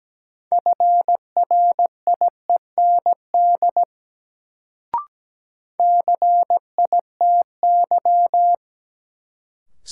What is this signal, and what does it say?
Morse code, heard as a keyed tone, about 17 wpm, 700 Hz. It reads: FRIEND CITY
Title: Sentences from Top 300 Words - New Word Drill - 17wpm